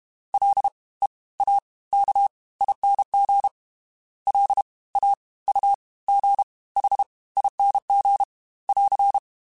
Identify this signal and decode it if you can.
Morse code, heard as a keyed tone, about 32 wpm, spaced as for 21 wpm, 800 Hz. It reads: LEAKING LAUGHING <AR>
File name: Special INGs / Special INGs-mp3s / Fast ing @30